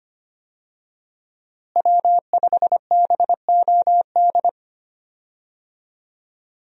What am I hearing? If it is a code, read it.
Morse code, heard as a keyed tone, about 25 wpm, 700 Hz. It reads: W5BOD
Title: USA Call Signs - 5 Characters - Repeated 3x - 25wpm